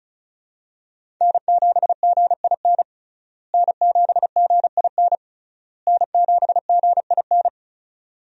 Morse code, heard as a keyed tone, about 36 wpm, 700 Hz. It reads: N7GIN N7GIN N7GIN
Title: USA Call Signs - 5 Characters - Repeated 3x - 35wpm